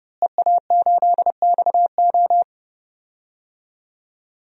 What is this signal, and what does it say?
Morse code, heard as a keyed tone, about 30 wpm, 700 Hz. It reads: EA8XO
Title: DXCC-100 Call Signs - 5 Characters - Repeated 3x - 30wpm